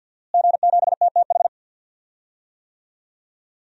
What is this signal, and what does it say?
Morse code, heard as a keyed tone, about 52 wpm, 700 Hz. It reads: G7TTH
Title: DXCC-100 Call Signs - 5 Characters - Repeated 3x - 50wpm